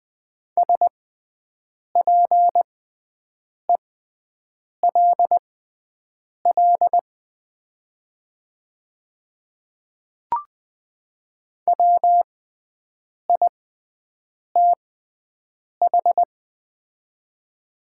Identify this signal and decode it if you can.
Morse code, heard as a keyed tone, about 20 wpm, spaced as for 7 wpm, 700 Hz. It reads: SPELL WITH
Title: 6x Character Spacing - Top 200 Words - 20wpm